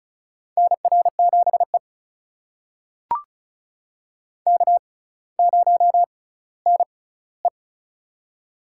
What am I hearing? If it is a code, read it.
Morse code, heard as a keyed tone, about 36 wpm, spaced as for 18 wpm, 700 Hz. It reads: NR7E K0NE
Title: USA Call Signs - 4 Characters - 6x Character Spacing - 35wpm